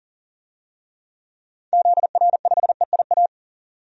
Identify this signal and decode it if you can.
Morse code, heard as a keyed tone, about 40 wpm, 700 Hz. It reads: ZR5EIA
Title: DXCC-100 Call Signs - 6 Characters - 40wpm